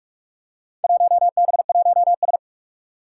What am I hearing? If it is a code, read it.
Morse code, heard as a keyed tone, about 46 wpm, 700 Hz. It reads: 1B1S
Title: Four Challenging Letter-Number Combinations 45wpm